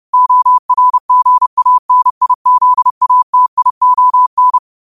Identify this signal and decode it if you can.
Morse code, heard as a keyed tone, about 30 wpm, 1000 Hz. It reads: ORGANIZATION
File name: 30wpm